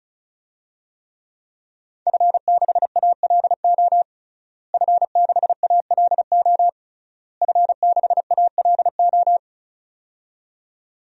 Morse code, heard as a keyed tone, about 36 wpm, 700 Hz. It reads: F6ALO F6ALO F6ALO
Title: DXCC-100 Call Signs - 5 Characters - Repeated 3x - 35wpm